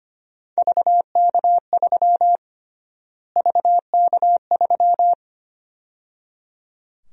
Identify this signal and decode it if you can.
Morse code, heard as a keyed tone, about 25 wpm, 700 Hz. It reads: VK3 VK3
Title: USA Call Sign Prefixes - Repeated 2x - 25wpm